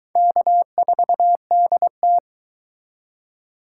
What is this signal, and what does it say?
Morse code, heard as a keyed tone, about 23 wpm, 700 Hz. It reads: K4DT